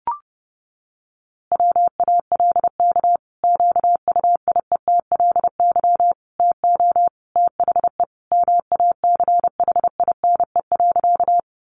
Morse code, heard as a keyed tone, about 30 wpm, 700 Hz. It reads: WALK QUIETLY TO THE MACHINE.
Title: A Sound Of Thunder 30wpm